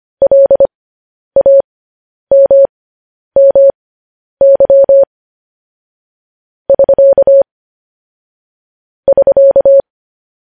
Morse code, heard as a keyed tone, about 25 wpm, spaced as for 10 wpm, 550 Hz. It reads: LAMMY <SK> <SK>
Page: news headlines in morse code at 10 wpm